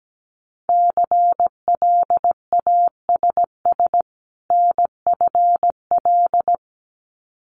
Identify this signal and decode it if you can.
Morse code, heard as a keyed tone, about 17 wpm, 700 Hz. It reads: CLASS NFL
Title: Winter Field Day Exchanges 17wpm